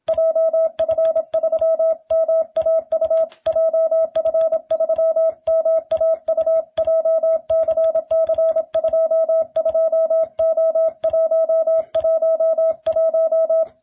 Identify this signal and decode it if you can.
Morse code, heard as a keyed tone, about 27 wpm, 650 Hz. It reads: JF3MAU JF3MAU JCC22O111